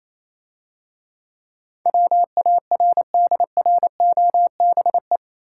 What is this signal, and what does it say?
Morse code, heard as a keyed tone, about 27 wpm, 700 Hz. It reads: WARDROBE
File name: Easy Words - Speed-Racing with Hints - 22wpm